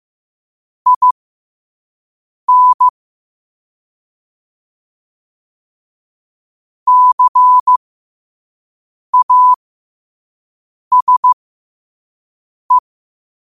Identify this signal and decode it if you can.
Morse code, heard as a keyed tone, about 15 wpm, spaced as for 5 wpm, 1000 Hz. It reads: IN CASE
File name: qotdfiles / 5wpm